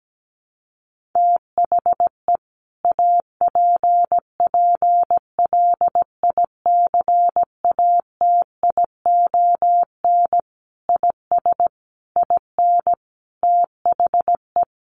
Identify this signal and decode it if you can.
Morse code, heard as a keyed tone, about 17 wpm, 700 Hz. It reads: THE APPLICATION IS IN THE
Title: Sentences from Top 1200 Words 17wpm